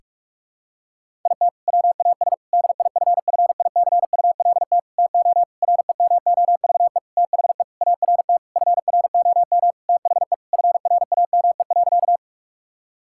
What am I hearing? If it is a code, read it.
Morse code, heard as a keyed tone, about 46 wpm, 700 Hz. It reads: IT WAS DIFFICULT TO REMOVE THE ART FROM THE FRAME.
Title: Sentences from Top 1000 Words - Review Part 6 - 45wpm